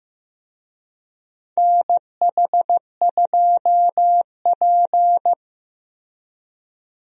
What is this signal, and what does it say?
Morse code, heard as a keyed tone, about 15 wpm, 700 Hz. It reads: NH2P
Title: USA Call Signs - 4 Characters - 15wpm